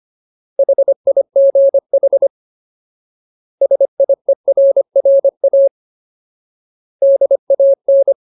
Morse code, heard as a keyed tone, about 25 wpm, 550 Hz. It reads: HIGH SIERRA DAN
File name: Ultimate Word List / 25WPM Ultimate